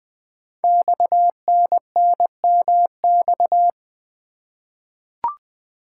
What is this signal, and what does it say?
Morse code, heard as a keyed tone, about 20 wpm, 700 Hz. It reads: XNNMX
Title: Code Groups - 5 Letters - 20wpm